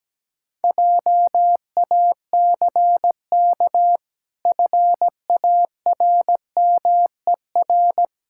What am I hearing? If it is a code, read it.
Morse code, heard as a keyed tone, about 17 wpm, 700 Hz. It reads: JACK FARMER
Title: Pangram Challenge v2 - 17wpm